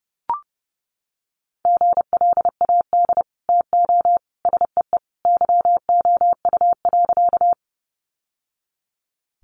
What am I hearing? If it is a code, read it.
Morse code, heard as a keyed tone, about 30 wpm, 700 Hz. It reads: GLAD TO SEE YOU.
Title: Sentences from Top 800 Words 30wpm